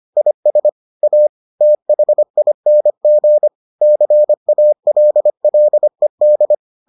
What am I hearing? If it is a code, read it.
Morse code, heard as a keyed tone, about 25 wpm, 600 Hz. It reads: IS A THING CALLED